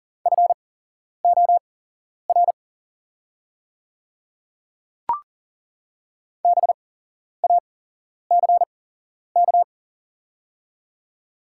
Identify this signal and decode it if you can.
Morse code, heard as a keyed tone, about 40 wpm, spaced as for 11 wpm, 700 Hz. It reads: FOR BACK
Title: 8x Character Spacing - Top 200 Words - 40wpm